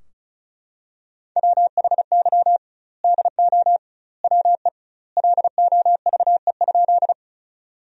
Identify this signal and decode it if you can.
Morse code, heard as a keyed tone, about 36 wpm, 700 Hz. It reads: WHY DO WE LOVE?
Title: 2x Word-Spacing - Sentences from Top 400 Words - 35wpm